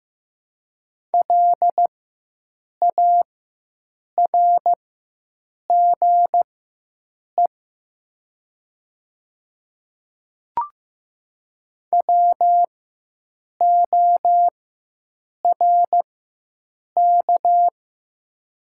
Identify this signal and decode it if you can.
Morse code, heard as a keyed tone, about 15 wpm, spaced as for 7 wpm, 700 Hz. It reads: LARGE WORK
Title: 4x Character Spacing - Top 200 Words - 15wpm